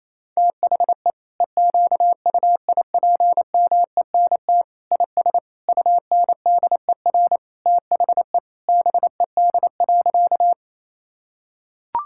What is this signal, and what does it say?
Morse code, heard as a keyed tone, about 27 wpm, 700 Hz. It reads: THE EQUIPMENT IS UNDER THE BED.
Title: Sentences from Top 1000 Words - Review Part 5 - 28wpm